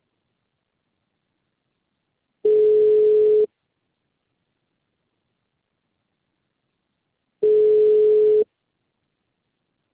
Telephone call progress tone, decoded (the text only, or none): ringback tone